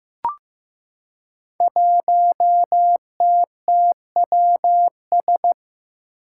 Morse code, heard as a keyed tone, about 15 wpm, 700 Hz. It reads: 1TTWS
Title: Five Challenging Letter-Number Combinations 15wpm